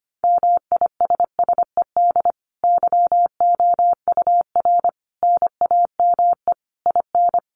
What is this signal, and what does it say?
Morse code, heard as a keyed tone, about 25 wpm, 700 Hz. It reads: MISSED YOUR NAME IN
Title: Common QSO Phrases 25wpm